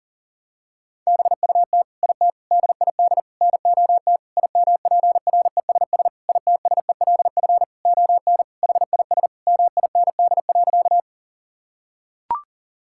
Morse code, heard as a keyed tone, about 37 wpm, 700 Hz. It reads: BUT IT DID NOT IMPRESS ITSELF ON HIS MIND.
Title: Sentences from Top 1600 Words 40wpm